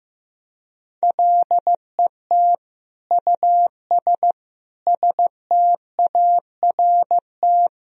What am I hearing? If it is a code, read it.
Morse code, heard as a keyed tone, about 16 wpm, 700 Hz. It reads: LET US START
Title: Sentences from Top 1000 Words - Review Part 1 - 15wpm